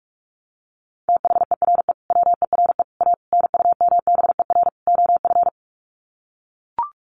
Morse code, heard as a keyed tone, about 44 wpm, 700 Hz. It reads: THERE WERE A NUMBER OF
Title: Sets of 5 Words Encore2 45wpm